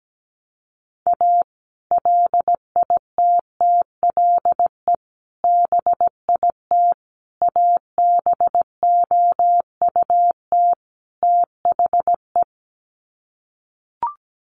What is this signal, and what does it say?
Morse code, heard as a keyed tone, about 17 wpm, 700 Hz. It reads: A LITTLE BIT ABOUT THE
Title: Sets of 5 Words Encore2 17wpm